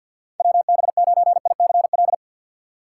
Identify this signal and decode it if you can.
Morse code, heard as a keyed tone, about 52 wpm, 700 Hz. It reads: WB9ICL
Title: USA Call Signs - 6 Characters - 4x Character Spacing - 50wpm